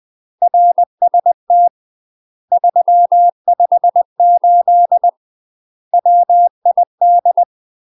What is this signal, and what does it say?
Morse code, heard as a keyed tone, about 20 wpm, 700 Hz. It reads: RST 358 WID
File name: UK MASTER QSOs TRAINING/20WPM